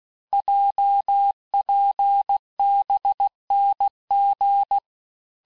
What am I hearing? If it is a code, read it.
Morse code, heard as a keyed tone, about 16 wpm, 800 Hz. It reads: JPBNG
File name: cours-du-lundi files / cours-1592--teste-chiffre-vitesse-15mpm